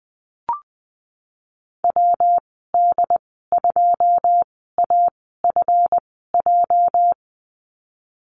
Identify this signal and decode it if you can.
Morse code, heard as a keyed tone, about 20 wpm, spaced as for 15 wpm, 700 Hz. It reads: WD2AFJ
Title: 238 Lesson 38g - Letter q - Call Signs 20wpm